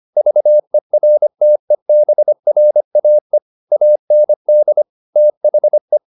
Morse code, heard as a keyed tone, about 25 wpm, 600 Hz. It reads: VERTEBRAE AND THE